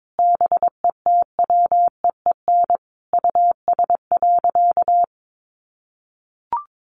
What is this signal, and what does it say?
Morse code, heard as a keyed tone, about 22 wpm, 700 Hz. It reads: BETWEEN US.
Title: Sentences from Top 800 Words 22wpm